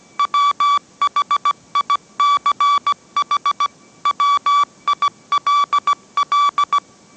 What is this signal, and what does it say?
Morse code, heard as a keyed tone, about 17 wpm, 1150 Hz. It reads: WHICH WILL
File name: mrx / mrx080 18wpm